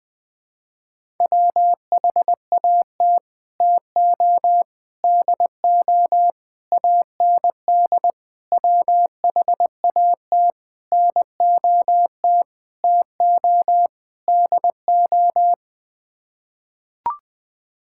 Morse code, heard as a keyed tone, about 21 wpm, 700 Hz. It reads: WHAT TO DO AND WHAT NOT TO DO
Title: Sets of 8 Words - 20wpm